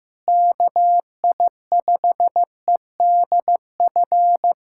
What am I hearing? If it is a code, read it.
Morse code, heard as a keyed tone, about 15 wpm, 700 Hz. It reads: KI5EDF
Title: USA Callsign DE Callsign 15wpm